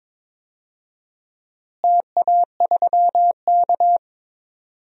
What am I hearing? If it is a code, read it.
Morse code, heard as a keyed tone, about 22 wpm, 700 Hz. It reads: TA3K
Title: DXCC-100 Call Signs - 4 Characters - Repeated 3x - 22wpm